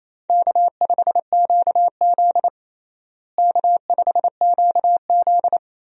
Morse code, heard as a keyed tone, about 28 wpm, 700 Hz. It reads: K5QZ K5QZ
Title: USA Call Signs - 4 Characters - Repeated 2x - 28wpm